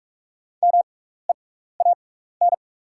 Morse code, heard as a keyed tone, about 44 wpm, spaced as for 15 wpm, 690 Hz. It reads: MEAN
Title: 6x Character Spacing - Top 300 Words - 45wpm